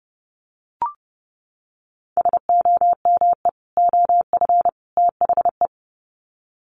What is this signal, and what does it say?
Morse code, heard as a keyed tone, about 30 wpm, 700 Hz. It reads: SOME OF THE
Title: Sets of 3 Words 30wpm